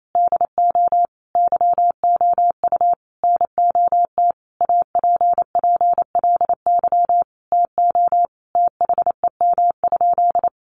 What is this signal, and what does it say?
Morse code, heard as a keyed tone, about 28 wpm, 700 Hz. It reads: DO YOU NOT APPLY TO THEM?
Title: Sentences from Top 1200 Words 28wpm